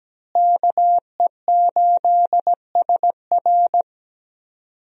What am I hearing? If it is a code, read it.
Morse code, heard as a keyed tone, about 17 wpm, 700 Hz. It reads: KE8SR
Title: USA Call Signs - 5 Characters - Repeated 3x - 17wpm